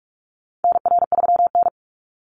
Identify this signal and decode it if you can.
Morse code, heard as a keyed tone, about 44 wpm, 700 Hz. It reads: NR3N
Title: Call Signs Repeated 3x - 45wpm